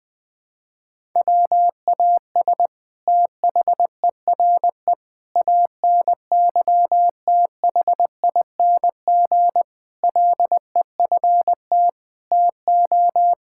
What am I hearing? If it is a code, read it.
Morse code, heard as a keyed tone, about 20 wpm, 700 Hz. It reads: WAS THERE ANYTHING LEFT TO
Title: Sentences from Top 1000 Words - Review Part 6 - 20wpm